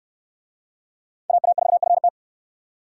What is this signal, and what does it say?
Morse code, heard as a keyed tone, about 69 wpm, 700 Hz. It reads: AA4FT